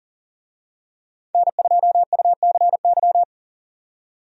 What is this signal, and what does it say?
Morse code, heard as a keyed tone, about 37 wpm, 700 Hz. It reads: N2UCY
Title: USA Call Signs - 5 Characters - 40wpm